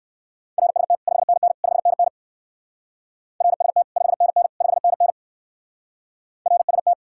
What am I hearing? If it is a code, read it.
Morse code, heard as a keyed tone, about 58 wpm, 700 Hz. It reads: RST 5NN 5NN RST 5NN 5NN RST